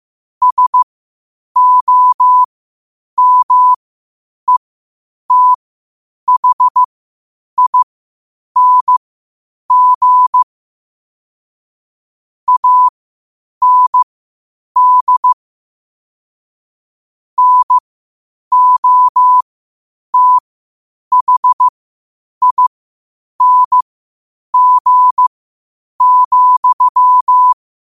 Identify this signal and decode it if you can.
Morse code, heard as a keyed tone, about 15 wpm, spaced as for 8 wpm, 1000 Hz. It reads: SOMETHING AND NOTHING,